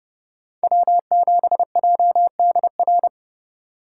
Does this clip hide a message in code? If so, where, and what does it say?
Yes: Morse code from 0.6 to 3.1 s, heard as a keyed tone, about 27 wpm, 700 Hz. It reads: W7JDR